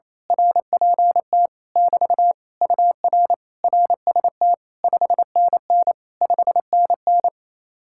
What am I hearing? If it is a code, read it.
Morse code, heard as a keyed tone, about 27 wpm, 700 Hz. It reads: RPT <BT> UR RST 5NN 5NN
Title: Speed-Racing Common QSO Phrases - 28wpm